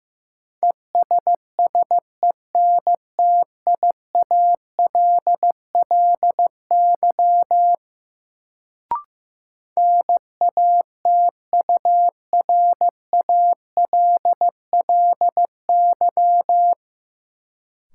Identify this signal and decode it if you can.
Morse code, heard as a keyed tone, about 15 wpm, 700 Hz. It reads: ESSENTIALLY NATURALLY
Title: Sentences from Top 1300 Words - New Word Drill - 15wpm